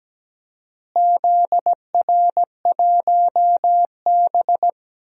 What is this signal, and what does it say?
Morse code, heard as a keyed tone, about 17 wpm, 700 Hz. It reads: ZR1B
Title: DXCC-100 Call Signs - 4 Characters - Repeated 2x - 17wpm